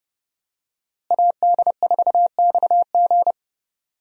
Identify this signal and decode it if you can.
Morse code, heard as a keyed tone, about 30 wpm, 700 Hz. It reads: AD4XG
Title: USA Call Signs - 5 Characters - Repeated 2x - 30wpm